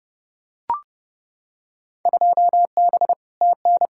Morse code, heard as a keyed tone, about 30 wpm, 700 Hz. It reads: 2B TN
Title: Field Day Exchanges Encore - 30wpm